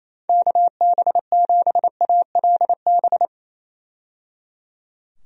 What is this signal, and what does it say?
Morse code, heard as a keyed tone, about 28 wpm, 700 Hz. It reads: KB7ALB
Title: USA Call Signs - 6 Characters - 28wpm